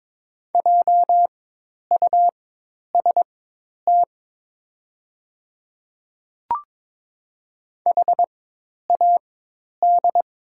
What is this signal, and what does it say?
Morse code, heard as a keyed tone, about 22 wpm, spaced as for 10 wpm, 700 Hz. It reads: JUST HAD
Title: 4x Character Spacing - Top 200 Words - 22wpm